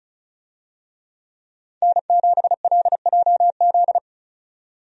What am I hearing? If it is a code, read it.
Morse code, heard as a keyed tone, about 36 wpm, 700 Hz. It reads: N7LJZ
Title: USA Call Signs - 5 Characters - 35wpm